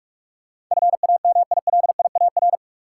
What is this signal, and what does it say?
Morse code, heard as a keyed tone, about 45 wpm, 700 Hz. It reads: FAMILIAR
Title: Sentences from Top 1000 Words - New Word Drill - 45wpm